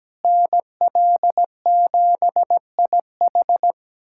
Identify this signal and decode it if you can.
Morse code, heard as a keyed tone, about 17 wpm, 700 Hz. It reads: NL7IH
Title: USA Callsign DE Callsign 17wpm